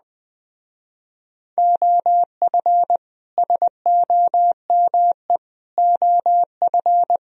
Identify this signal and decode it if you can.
Morse code, heard as a keyed tone, about 20 wpm, 700 Hz. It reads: OF SOME OF